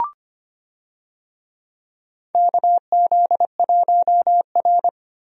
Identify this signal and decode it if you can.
Morse code, heard as a keyed tone, about 25 wpm, 700 Hz. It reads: KZ1R
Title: USA Call Signs - 4 Characters - 25wpm